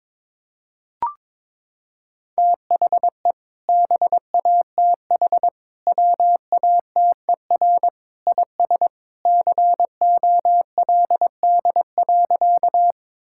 Morse code, heard as a keyed tone, about 22 wpm, 700 Hz. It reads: THE BATH WATER IS COLD.
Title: Sentences from Top 1600 Words 22wpm